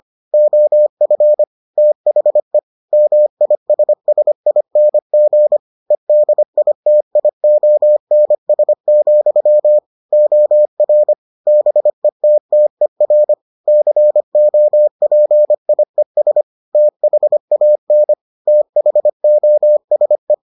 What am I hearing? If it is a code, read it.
Morse code, heard as a keyed tone, about 25 wpm, 600 Hz. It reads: OF THE MISSING EDITIONS, OR BETTER COPIES THAN THOSE